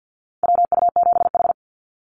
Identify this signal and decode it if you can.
Morse code, heard as a keyed tone, about 50 wpm, 700 Hz. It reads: WU7H
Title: NA SOTA Activator Callsigns 50wpm